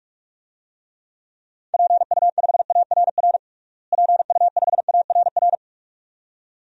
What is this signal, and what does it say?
Morse code, heard as a keyed tone, about 41 wpm, 700 Hz. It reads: PU5ARR PU5ARR
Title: DXCC-100 Call Signs - 6 Characters - Repeated 2x - 45wpm